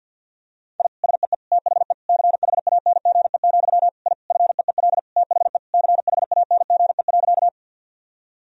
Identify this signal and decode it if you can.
Morse code, heard as a keyed tone, about 50 wpm, 700 Hz. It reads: I SEE THE CHANGE, I FEEL THE CHANGE.
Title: Sentences from Top 1000 Words - Review Part 1 - 50wpm